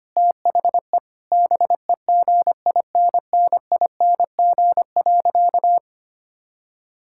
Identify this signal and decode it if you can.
Morse code, heard as a keyed tone, about 25 wpm, 700 Hz. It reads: THE BEGINNING.
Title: Sentences from Top 1000 Words - Review Part 5 - 25wpm